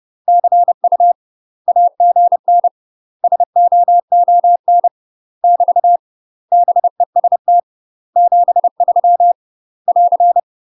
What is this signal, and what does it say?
Morse code, heard as a keyed tone, about 30 wpm, 700 Hz. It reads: CU AGN SOON <BT> BEST 73 <AR>
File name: UK MASTER QSOs TRAINING/30WPM